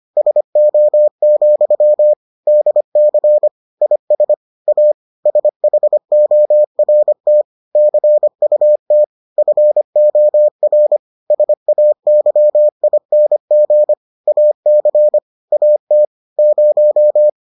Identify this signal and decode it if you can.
Morse code, heard as a keyed tone, about 25 wpm, 600 Hz. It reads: SO, DC IS A SHORT CUT FOR SAYING AC AT 0